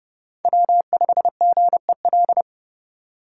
Morse code, heard as a keyed tone, about 29 wpm, 700 Hz. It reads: W5GEL